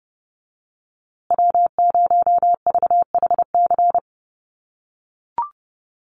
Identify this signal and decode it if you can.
Morse code, heard as a keyed tone, about 30 wpm, 700 Hz. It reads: W0VHC
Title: USA Call Signs Repeated 3x 30wpm